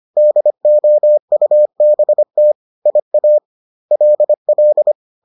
Morse code, heard as a keyed tone, about 25 wpm, spaced as for 22 wpm, 600 Hz. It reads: DOUBT IALL